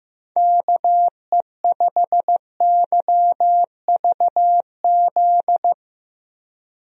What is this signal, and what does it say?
Morse code, heard as a keyed tone, about 15 wpm, 700 Hz. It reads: KE5YVZ
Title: USA Call Signs - 6 Characters - Repeated 3x - 15wpm